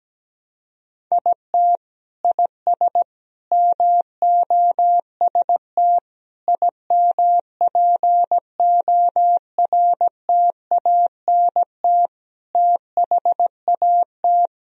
Morse code, heard as a keyed tone, about 17 wpm, 700 Hz. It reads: IT IS MOST IMPORTANT THAT
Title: Sentences from Top 1000 Words - Review Part 1 - 17wpm